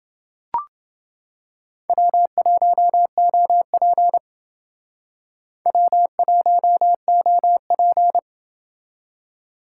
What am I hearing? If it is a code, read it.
Morse code, heard as a keyed tone, about 29 wpm, 700 Hz. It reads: W1OP W1OP